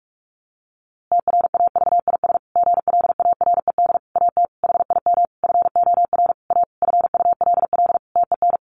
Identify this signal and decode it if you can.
Morse code, heard as a keyed tone, about 44 wpm, 700 Hz. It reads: TRAVIS GLARED AT HIM FOR A FULL TEN
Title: A Sound Of Thunder 45wpm